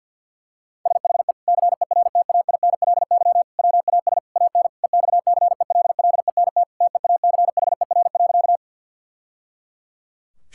Morse code, heard as a keyed tone, about 50 wpm, 700 Hz. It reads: SHE CERTAINLY WAS AN EXCELLENT TEACHER.
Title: Sentences from Top 1000 Words - Review Part 6 - 50wpm